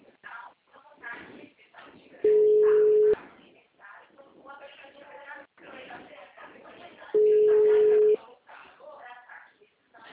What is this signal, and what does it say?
Ringback tone (Europe, 425 Hz), with speech.
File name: out- -2005- - - .266